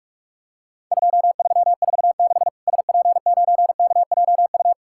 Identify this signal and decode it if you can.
Morse code, heard as a keyed tone, about 46 wpm, 700 Hz. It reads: 2346 SP9KJU
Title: Worked All Europe DX Contest - QTC Exchange - 45wpm